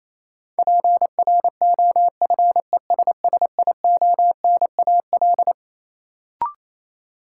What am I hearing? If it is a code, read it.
Morse code, heard as a keyed tone, about 28 wpm, 700 Hz. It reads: PROFESSIONAL
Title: Sentences from Top 1000 Words - Review Part 4 - New Word Drill - 28wpm